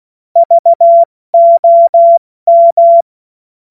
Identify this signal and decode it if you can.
Morse code, heard as a keyed tone, about 16 wpm, 680 Hz. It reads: VOM
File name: Morsekurs / Witze / Witz 16 April 2024 70 F80